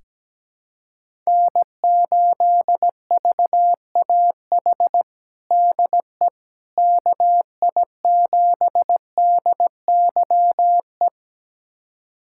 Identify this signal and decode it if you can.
Morse code, heard as a keyed tone, about 17 wpm, 700 Hz. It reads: N8VAH DE KI7DYE
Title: USA Callsign DE Callsign 17wpm